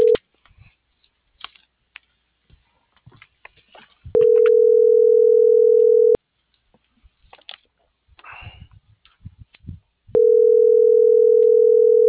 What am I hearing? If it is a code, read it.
Ringback tone (North America), 440 + 480 Hz.